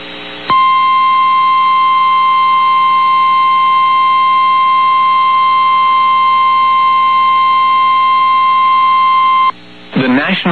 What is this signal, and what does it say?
Emergency Alert System (SAME) tones (the 1050 Hz attention tone), with speech. It is heard from 0.5 to 9.5 s.
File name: eas received alerts / L1 19561 audio